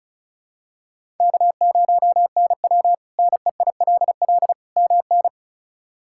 Morse code, heard as a keyed tone, about 34 wpm, 700 Hz. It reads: K0NW NEILL MN
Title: K1USN SST QSO Practice - 35wpm